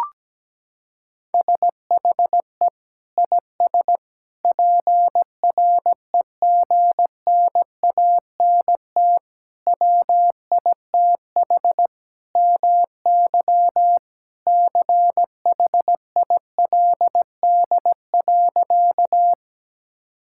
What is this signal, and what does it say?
Morse code, heard as a keyed tone, about 17 wpm, 700 Hz. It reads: SHE IS PREGNANT WITH MY CHILD.
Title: Sentences from Top 1000 Words - Review Part 6 - 17wpm